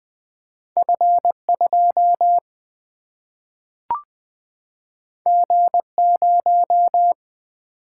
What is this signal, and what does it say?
Morse code, heard as a keyed tone, about 20 wpm, 700 Hz. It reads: F2 G0
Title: Single-Letter Single-Number 20wpm